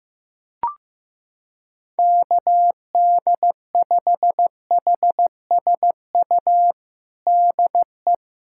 Morse code, heard as a keyed tone, about 16 wpm, 700 Hz. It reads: KD5HSU DE
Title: USA Callsign DE Callsign - 6 Characters - 15wpm